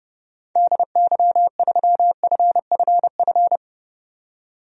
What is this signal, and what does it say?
Morse code, heard as a keyed tone, about 30 wpm, 700 Hz. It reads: DY3FFF
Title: DXCC-100 Call Signs - 6 Characters - 30wpm